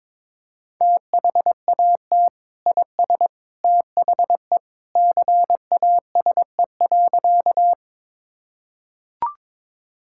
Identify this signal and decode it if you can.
Morse code, heard as a keyed tone, about 22 wpm, 700 Hz. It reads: THAT IS THE CASE.
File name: practice / 22wpm / Sentences with 2-Word Phrases - 22wpm